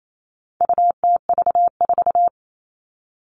Morse code, heard as a keyed tone, about 28 wpm, 700 Hz. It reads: UTV4
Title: All Cut Numbers - 4-Digit 28wpm